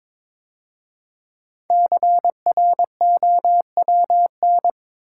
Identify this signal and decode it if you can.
Morse code, heard as a keyed tone, about 22 wpm, 700 Hz. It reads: CROWN
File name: 5 Letter Words - First 3 Letters Given - 22wpm